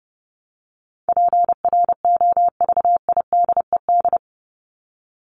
Morse code, heard as a keyed tone, about 30 wpm, 700 Hz. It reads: PROVIDED
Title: Top 1751-2000 Words 30wpm